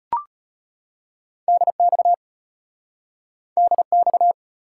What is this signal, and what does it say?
Morse code, heard as a keyed tone, about 34 wpm, 700 Hz. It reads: DX DX